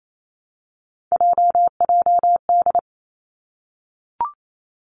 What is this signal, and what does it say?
Morse code, heard as a keyed tone, about 28 wpm, 700 Hz. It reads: JJD
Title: Three Challenging Letter-Number Combinations 28wpm